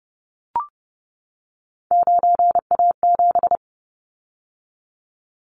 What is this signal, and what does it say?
Morse code, heard as a keyed tone, about 30 wpm, 700 Hz. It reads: 9A7
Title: Callsign Prefixes for Top 50 DXCC - 30wpm